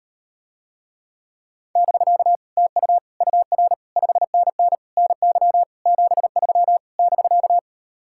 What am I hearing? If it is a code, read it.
Morse code, heard as a keyed tone, about 38 wpm, 700 Hz. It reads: <BK> TU UR 5NN NY 73 <BK>